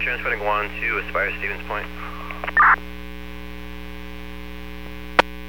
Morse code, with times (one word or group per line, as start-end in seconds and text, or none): none